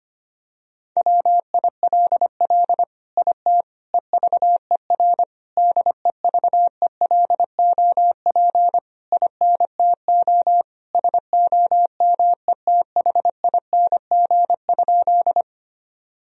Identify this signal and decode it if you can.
Morse code, heard as a keyed tone, about 25 wpm, 700 Hz. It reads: WILL IT EVER DEVELOP INTO SOMETHING?
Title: Sentences from Top 1000 Words - Review Part 2 - 25wpm